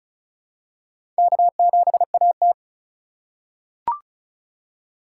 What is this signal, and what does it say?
Morse code, heard as a keyed tone, about 36 wpm, 700 Hz. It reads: K7AT
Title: USA Call Signs - 4 Characters - 2x Character Spacing - 35wpm